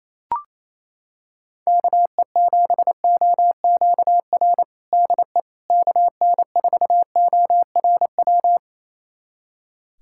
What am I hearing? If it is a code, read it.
Morse code, heard as a keyed tone, about 28 wpm, 700 Hz. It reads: KE7OQR DE KN4ORW
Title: USA Callsign DE Callsign - 6 Characters - 28wpm